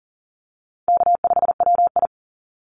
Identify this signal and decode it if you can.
Morse code, heard as a keyed tone, about 40 wpm, 700 Hz. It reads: K5WI
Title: NA SOTA Chaser Callsigns 40wpm